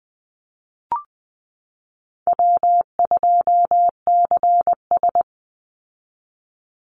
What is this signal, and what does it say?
Morse code, heard as a keyed tone, about 20 wpm, 700 Hz. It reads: W2CS
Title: Call Signs Repeated 3x - 20wpm